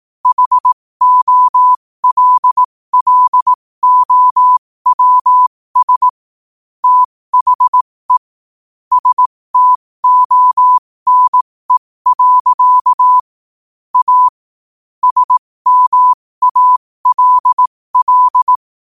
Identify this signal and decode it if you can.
Morse code, heard as a keyed tone, about 18 wpm, spaced as for 15 wpm, 1000 Hz. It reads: HOLLOWS THE STONE. A SMALL
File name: qotdfiles / 15wpm